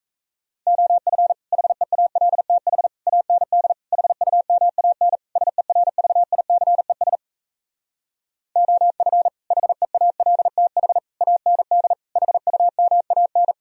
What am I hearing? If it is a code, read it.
Morse code, heard as a keyed tone, about 38 wpm, 700 Hz. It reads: OF HEALTH AND HUMAN SERVICES OF HEALTH AND HUMAN